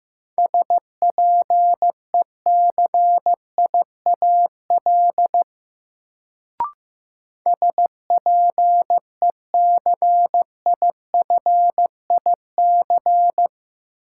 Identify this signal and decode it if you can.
Morse code, heard as a keyed tone, about 15 wpm, 700 Hz. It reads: SPECIAL SPECIFIC
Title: Sentences from Top 1000 Words - Review Part 3 - New Word Drill - 15wpm